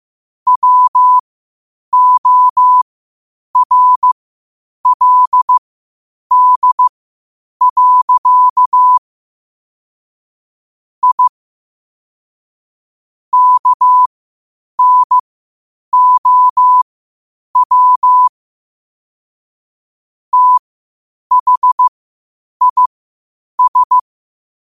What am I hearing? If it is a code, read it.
Morse code, heard as a keyed tone, about 15 wpm, spaced as for 8 wpm, 1000 Hz. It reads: WORLD. I KNOW THIS
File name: qotdfiles / 8wpm